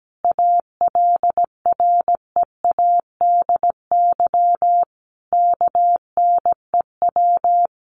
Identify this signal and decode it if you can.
Morse code, heard as a keyed tone, about 17 wpm, 700 Hz. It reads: ALREADY KNEW
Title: Sentences from Top 700 Words 17wpm